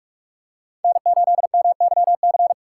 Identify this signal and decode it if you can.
Morse code, heard as a keyed tone, about 40 wpm, 700 Hz. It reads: N8MYC